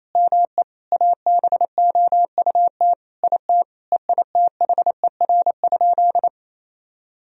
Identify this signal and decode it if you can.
Morse code, heard as a keyed tone, about 28 wpm, 700 Hz. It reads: ME ABOUT IT EITHER?
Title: Sentences from Top 1000 Words - Review Part 4 - 28wpm